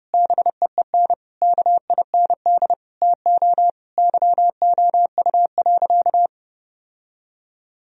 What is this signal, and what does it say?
Morse code, heard as a keyed tone, about 30 wpm, 700 Hz. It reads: BEEN KIND TO YOU.
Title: Sentences from Top 1000 Words - Review Part 2 - 30wpm